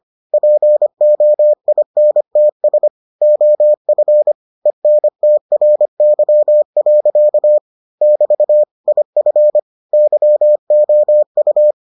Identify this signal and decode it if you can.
Morse code, heard as a keyed tone, about 25 wpm, 600 Hz. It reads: POINTS OF ENTRY. <BT> IF YOU